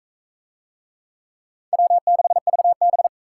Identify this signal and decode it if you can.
Morse code, heard as a keyed tone, about 41 wpm, 700 Hz. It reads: W6VB